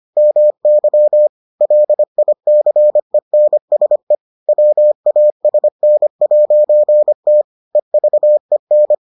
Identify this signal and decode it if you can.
Morse code, heard as a keyed tone, about 25 wpm, 600 Hz. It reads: MY LICENSE WASN'T EVEN